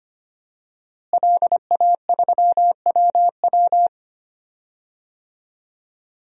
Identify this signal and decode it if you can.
Morse code, heard as a keyed tone, about 25 wpm, 700 Hz. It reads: LA3WW
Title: DXCC-100 Call Signs - 5 Characters - Repeated 2x - 25wpm